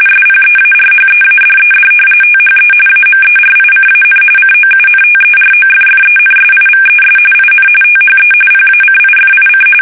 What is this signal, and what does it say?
RTTY, about 45.45 baud. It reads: SPEARHEADED THE AMERICAN COUNTEROFFENSIVES IN THE PACIFIC
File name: fsk850 / battleshipnj